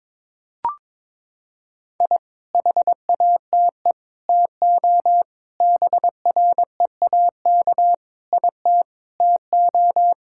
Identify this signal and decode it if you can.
Morse code, heard as a keyed tone, about 21 wpm, 700 Hz. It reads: I HATE TO BREAK IT TO
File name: practice / 22wpm / Sentences with 8-Word Phrases - 22wpm